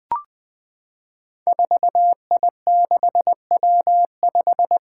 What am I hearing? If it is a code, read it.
Morse code, heard as a keyed tone, about 20 wpm, 700 Hz. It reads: 4I6W5
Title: Five Challenging Letter-Number Combinations 20wpm